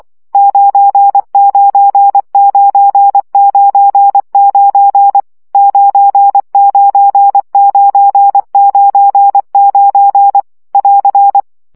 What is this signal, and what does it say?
Morse code, heard as a keyed tone, about 24 wpm, 800 Hz. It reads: 99999 99999 <AR>
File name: CW gyakorlas 9 20 baud